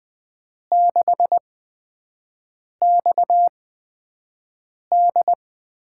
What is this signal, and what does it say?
Morse code, heard as a keyed tone, about 20 wpm, spaced as for 5 wpm, 700 Hz. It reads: 6XD
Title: USA Call Signs - 5 Characters - 8x Character Spacing - 20wpm